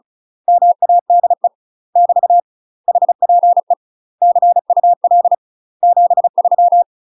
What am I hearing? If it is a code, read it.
Morse code, heard as a keyed tone, about 35 wpm, 700 Hz. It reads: MADE <BT> HPE CUL 73
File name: UK MASTER QSOs TRAINING / 35WPM / UK MASTER QSOs 0026